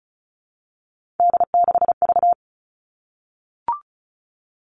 Morse code, heard as a keyed tone, about 36 wpm, 700 Hz. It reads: D6V
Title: Three Challenging Letter-Number Combinations 35wpm